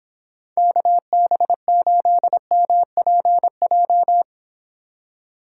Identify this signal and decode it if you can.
Morse code, heard as a keyed tone, about 25 wpm, 700 Hz. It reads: KB8MPJ